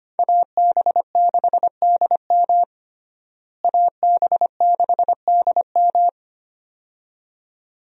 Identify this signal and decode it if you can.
Morse code, heard as a keyed tone, about 25 wpm, 700 Hz. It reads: AB6DM AB6DM
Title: USA Call Signs - 5 Characters - Repeated 2x - 25wpm